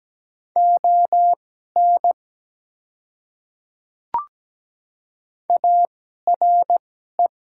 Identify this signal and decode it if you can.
Morse code, heard as a keyed tone, about 17 wpm, spaced as for 12 wpm, 700 Hz. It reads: ON ARE
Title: 2x Character Spacing - Top 100 Words - 17wpm